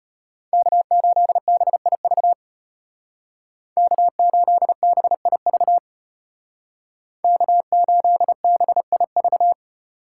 Morse code, heard as a keyed tone, about 33 wpm, 700 Hz. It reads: K8BIV K8BIV K8BIV